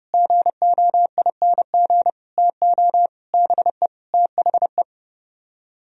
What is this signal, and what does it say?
Morse code, heard as a keyed tone, about 39 wpm, 700 Hz. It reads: GOING TO BE THE